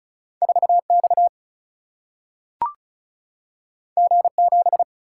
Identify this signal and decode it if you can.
Morse code, heard as a keyed tone, about 36 wpm, 700 Hz. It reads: 4 X G 7
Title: Single-Letter Single-Number 35wpm